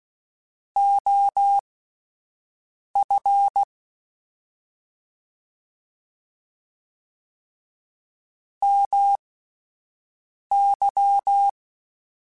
Morse code, heard as a keyed tone, about 16 wpm, spaced as for 5 wpm, 800 Hz. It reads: OF MY
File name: QSOs / QSOs-mp3s / WD0FBT 1@10